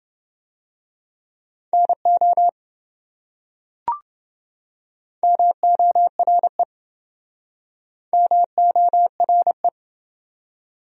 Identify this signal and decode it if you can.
Morse code, heard as a keyed tone, about 27 wpm, 700 Hz. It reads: NO MORE MORE